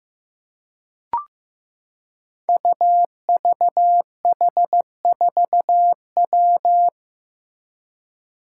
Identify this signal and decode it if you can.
Morse code, heard as a keyed tone, about 15 wpm, 700 Hz. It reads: UVH4W
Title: Five Challenging Letter-Number Combinations 15wpm